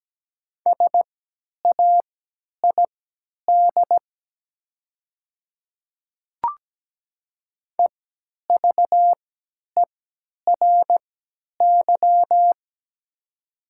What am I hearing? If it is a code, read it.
Morse code, heard as a keyed tone, about 17 wpm, spaced as for 10 wpm, 700 Hz. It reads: SAID EVERY
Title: 3x Character Spacing - Top 200 Words - 17wpm